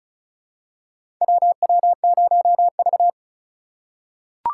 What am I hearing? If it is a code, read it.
Morse code, heard as a keyed tone, about 36 wpm, 700 Hz. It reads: WW0V
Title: USA Call Signs - 4 Characters - 3x Character Spacing - 35wpm